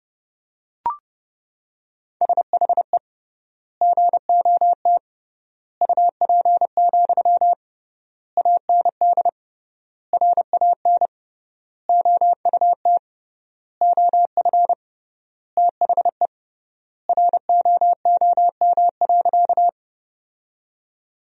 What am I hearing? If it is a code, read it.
Morse code, heard as a keyed tone, about 30 wpm, 700 Hz. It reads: SHE GOT UP, AND RAN OUT OF THE ROOM.
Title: 3x Word-Spacing - Sentences from Top 500 Words - 30wpm